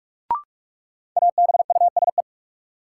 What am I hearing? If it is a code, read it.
Morse code, heard as a keyed tone, about 46 wpm, 700 Hz. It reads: ABUSE
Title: Sentences from Top 1300 Words - New Word Drill - 45wpm